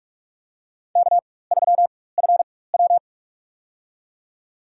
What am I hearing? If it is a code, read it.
Morse code, heard as a keyed tone, about 47 wpm, spaced as for 21 wpm, 700 Hz. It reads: K3FW
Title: USA Call Signs - 4 Characters - 4x Character Spacing - 45wpm